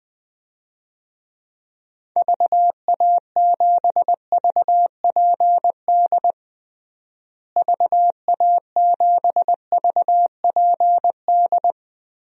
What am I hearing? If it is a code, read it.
Morse code, heard as a keyed tone, about 21 wpm, 700 Hz. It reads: VA7VPD VA7VPD
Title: DXCC-100 Call Signs - 6 Characters - Repeated 3x - 20wpm